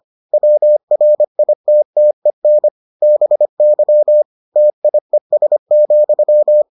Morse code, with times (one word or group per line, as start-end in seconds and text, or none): 0.3-2.7 WRITTEN
3.0-4.2 BY
4.6-6.6 TIES,